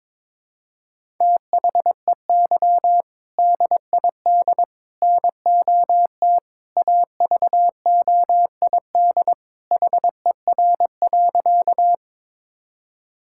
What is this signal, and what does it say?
Morse code, heard as a keyed tone, about 22 wpm, 700 Hz. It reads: THEY DID NOT AVOID HER.
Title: Sentences from Top 1000 Words - Review Part 6 - 22wpm